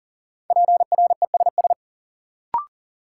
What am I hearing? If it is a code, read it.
Morse code, heard as a keyed tone, about 40 wpm, 700 Hz. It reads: PRESS
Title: Sentences from Top 800 Words - New Word Drill - 40wpm